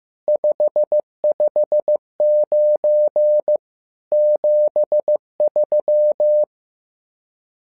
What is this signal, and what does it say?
Morse code, heard as a keyed tone, about 15 wpm, 600 Hz. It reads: 559 73
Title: SOTA QSO Practice 15wpm